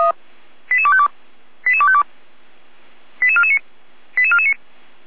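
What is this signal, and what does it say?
Five-tone selective call (ZVEI), heard as quick runs of five tones, tones of about 70 ms. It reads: 88252 88252 88449 88449
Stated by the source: ZVEI five-tone selective call received on SDR.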